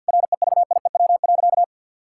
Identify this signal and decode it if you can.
Morse code, heard as a keyed tone, about 50 wpm, 700 Hz. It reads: REVIEW.
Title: Sentences from Top 1000 Words - Review Part 6 - 50wpm